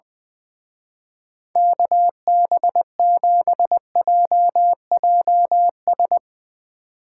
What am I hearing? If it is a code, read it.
Morse code, heard as a keyed tone, about 20 wpm, 700 Hz. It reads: KB7JJS
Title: Speed Racing - USA 2x3 Call Signs 20wpm